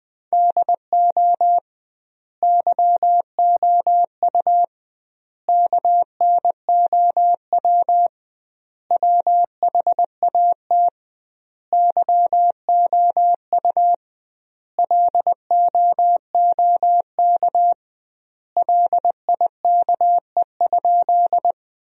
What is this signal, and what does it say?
Morse code, heard as a keyed tone, about 20 wpm, 700 Hz. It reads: DO YOU KNOW WHAT YOU LOOK LIKE?
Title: 2x Word-Spacing - Sentences from Top 100 Words - 20wpm